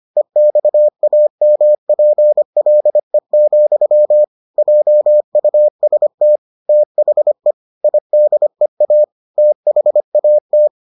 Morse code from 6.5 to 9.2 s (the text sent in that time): THE IDEA